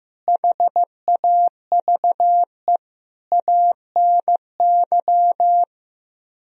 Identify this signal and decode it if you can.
Morse code, heard as a keyed tone, about 15 wpm, 700 Hz. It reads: HAVE ANY
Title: Speed-Racing - Sets of 5 Words 15wpm